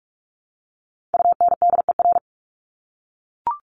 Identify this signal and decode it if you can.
Morse code, heard as a keyed tone, about 44 wpm, 700 Hz. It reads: UNDER
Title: Top 200 Words Repeated 45wpm